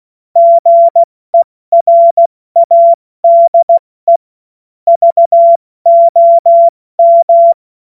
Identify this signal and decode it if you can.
Morse code, heard as a keyed tone, about 16 wpm, 680 Hz. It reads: GERADE VOM
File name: Morsekurs / Witze / Witz 29. Juni 2021, 70 F80